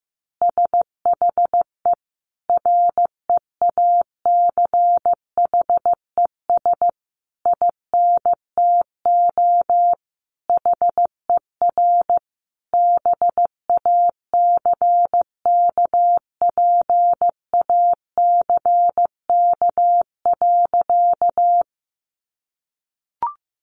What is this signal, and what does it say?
Morse code, heard as a keyed tone, about 15 wpm, 700 Hz. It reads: SHE REACHES INTO HER BACKPACK.
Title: Additional Easy Phrases 15wpm